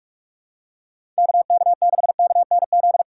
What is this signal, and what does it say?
Morse code, heard as a keyed tone, about 47 wpm, 700 Hz. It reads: KK6KNZ
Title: USA Call Signs - 4x Character Spacing - 45wpm